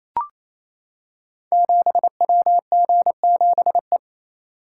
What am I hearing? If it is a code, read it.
Morse code, heard as a keyed tone, about 27 wpm, 700 Hz. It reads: 7WG7E
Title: Code Groups - 5 Letter:Numbers - 28wpm